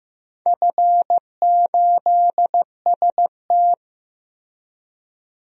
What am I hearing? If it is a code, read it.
Morse code, heard as a keyed tone, about 15 wpm, 700 Hz. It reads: F8ST
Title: DXCC-100 Call Signs - 4 Characters - Repeated 3x - 15wpm